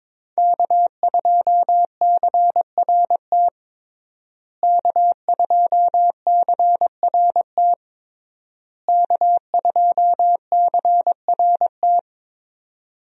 Morse code, heard as a keyed tone, about 22 wpm, 700 Hz. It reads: K2CRT K2CRT K2CRT
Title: USA Call Signs - 5 Characters - Repeated 3x - 22wpm